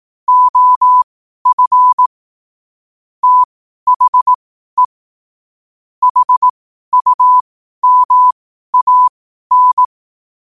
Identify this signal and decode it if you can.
Morse code, heard as a keyed tone, about 18 wpm, spaced as for 12 wpm, 1000 Hz. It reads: OF THE HUMAN
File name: qotdfiles / 12wpm